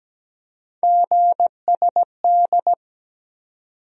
Morse code, heard as a keyed tone, about 17 wpm, 700 Hz. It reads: GSD
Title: USA Call Sign Postfixes - 3 Letters 17wpm